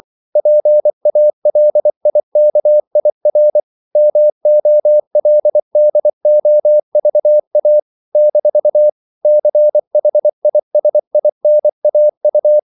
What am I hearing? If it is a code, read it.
Morse code, heard as a keyed tone, about 24 wpm, 600 Hz. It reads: PALIKIR MOLDOVA - CHISINAU
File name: CW / Practice / NationalCapitals24